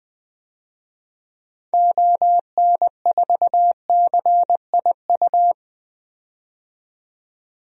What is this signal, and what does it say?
Morse code, heard as a keyed tone, about 20 wpm, 700 Hz. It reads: ON4CIU
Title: DXCC-100 Call Signs - 6 Characters - 20wpm